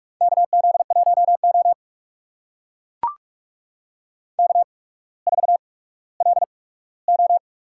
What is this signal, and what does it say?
Morse code, heard as a keyed tone, about 44 wpm, spaced as for 19 wpm, 700 Hz. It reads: KZ1O K4LO
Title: USA Call Signs - 4 Characters - 8x Character Spacing - 45wpm